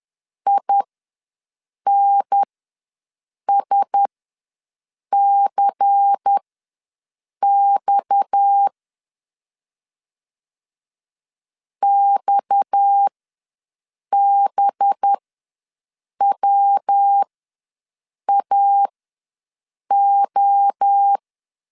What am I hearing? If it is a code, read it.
Morse code, heard as a keyed tone, about 10 wpm, spaced as for 6 wpm, 800 Hz. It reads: INSCX XBWAO
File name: LEZIONI / 6 PKX0 / 6 PKX ws 10wpm 3706